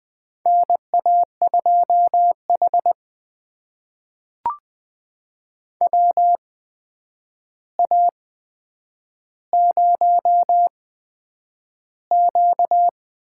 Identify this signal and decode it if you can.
Morse code, heard as a keyed tone, about 20 wpm, spaced as for 9 wpm, 700 Hz. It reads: NA2H WA0Q
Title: USA Call Signs - 4 Characters - 8x Character Spacing - 20wpm